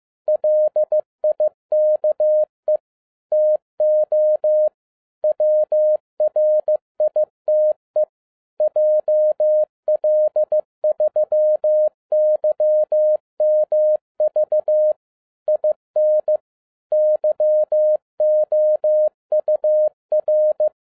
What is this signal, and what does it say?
Morse code, heard as a keyed tone, about 15 wpm, 610 Hz. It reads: LIKE TO WRITE JL3YMV IN YOUR
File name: 2011-10-mp3 / n12-JL3YMV 15wpm